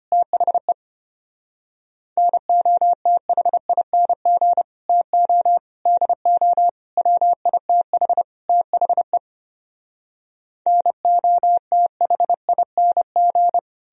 Morse code, heard as a keyed tone, about 33 wpm, 700 Hz. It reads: THE NOTHING TO DO WITH THE NOTHING